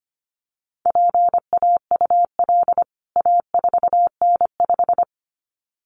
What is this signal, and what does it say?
Morse code, heard as a keyed tone, about 25 wpm, 700 Hz. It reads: PAUL A4N5
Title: CWT Exchange 25wpm